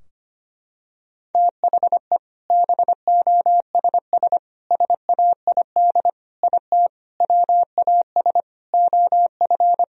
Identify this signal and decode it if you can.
Morse code, heard as a keyed tone, about 25 wpm, 700 Hz. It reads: THE BOSS SAID IT WAS OF
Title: Sentences from Top 1000 Words - Review Part 4 - 25wpm